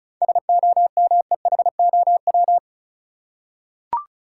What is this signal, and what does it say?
Morse code, heard as a keyed tone, about 36 wpm, 700 Hz. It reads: SOMEHOW
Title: Seven Letter Words - 35wpm